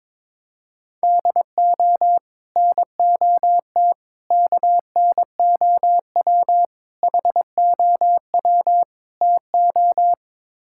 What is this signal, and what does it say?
Morse code, heard as a keyed tone, about 22 wpm, 700 Hz. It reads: DO NOT KNOW HOW TO
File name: practice / 22wpm / Build A Phrase - 5-Word Phrases - 22wpm